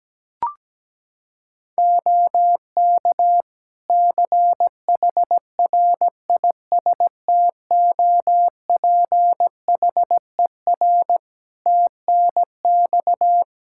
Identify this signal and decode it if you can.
Morse code, heard as a keyed tone, about 17 wpm, 700 Hz. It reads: OK CHRISTOPHER TNX
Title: Practice QSO - Exchange 4 - Ending - 17wpm